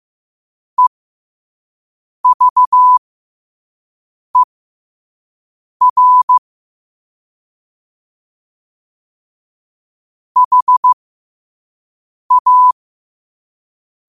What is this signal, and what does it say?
Morse code, heard as a keyed tone, about 15 wpm, spaced as for 5 wpm, 1000 Hz. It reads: EVER HA
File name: qotdfiles / 5wpm